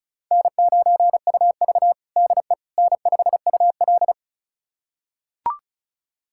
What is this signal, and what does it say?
Morse code, heard as a keyed tone, about 36 wpm, 700 Hz. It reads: N9UV DE N5UL
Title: USA Callsign DE Callsign - 4 Characters - 35wpm